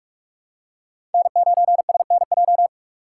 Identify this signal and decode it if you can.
Morse code, heard as a keyed tone, about 46 wpm, 700 Hz. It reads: N9SNJ
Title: USA Call Signs - 5 Characters - 2x Character Spacing - 45wpm